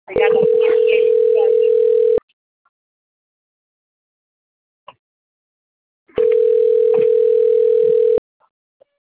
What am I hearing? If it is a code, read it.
Ringback tone (North America), 440 + 480 Hz, with speech.
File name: rg-1600-unknown- - - .34945